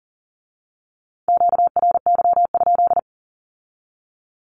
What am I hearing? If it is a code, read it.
Morse code, heard as a keyed tone, about 38 wpm, 700 Hz. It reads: QRY?
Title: Q-Codes with Spoken Meaning - 40wpm